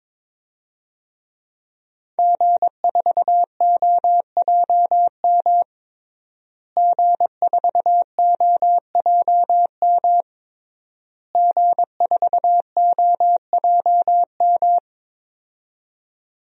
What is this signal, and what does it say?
Morse code, heard as a keyed tone, about 22 wpm, 700 Hz. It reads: G4OJM G4OJM G4OJM
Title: DXCC-100 Call Signs - 5 Characters - Repeated 3x - 22wpm